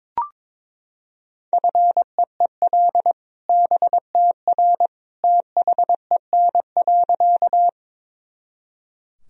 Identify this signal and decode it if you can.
Morse code, heard as a keyed tone, about 22 wpm, 700 Hz. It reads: FEEL BTR THEN.
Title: Sentences from Top 400 Words and Abbreviations 22wpm